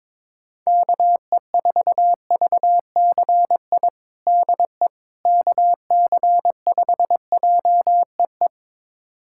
Morse code, heard as a keyed tone, about 22 wpm, 700 Hz. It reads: KE4VCI DE KC5JEE
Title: USA Callsign DE Callsign - 6 Characters - 22wpm